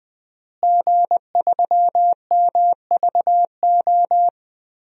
Morse code, heard as a keyed tone, about 20 wpm, 700 Hz. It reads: G3MVO
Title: DXCC-100 Call Signs - 5 Characters - Repeated 2x - 20wpm